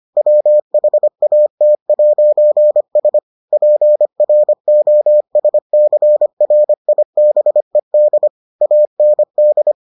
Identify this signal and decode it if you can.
Morse code, heard as a keyed tone, about 25 wpm, 600 Hz. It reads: WHAT'S PROSCRIBED AND